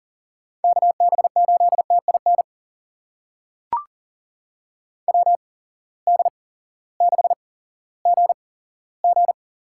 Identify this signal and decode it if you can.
Morse code, heard as a keyed tone, about 40 wpm, spaced as for 18 wpm, 700 Hz. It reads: KB8TIN WD6GG
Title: USA Call Signs - 6 Characters - 8x Character Spacing - 40wpm